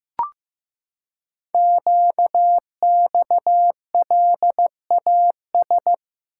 Morse code, heard as a keyed tone, about 15 wpm, 700 Hz. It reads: QXLAS
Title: Code Groups - 5 Letters - 15wpm